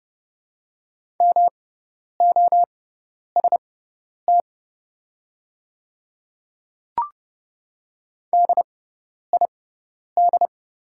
Morse code, heard as a keyed tone, about 30 wpm, spaced as for 10 wpm, 700 Hz. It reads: MOST DID
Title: 6x Character Spacing - Top 100 Words - 30wpm